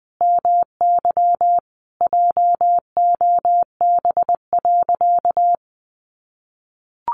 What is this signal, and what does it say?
Morse code, heard as a keyed tone, about 19 wpm, 700 Hz. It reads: MY JOB.
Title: Sentences from Top 600 Words 20wpm